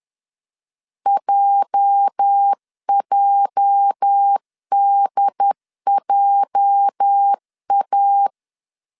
Morse code, heard as a keyed tone, about 10 wpm, 800 Hz. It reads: JJDJA
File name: LEZIONI / 8 YJ45 / 8 YJ 10wpm 2651